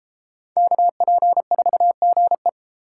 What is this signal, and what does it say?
Morse code, heard as a keyed tone, about 32 wpm, 700 Hz. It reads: KP4GE